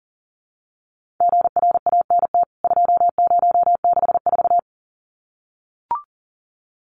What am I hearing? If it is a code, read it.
Morse code, heard as a keyed tone, about 40 wpm, 700 Hz. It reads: GRANT 2064
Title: CWT Exchange 40wpm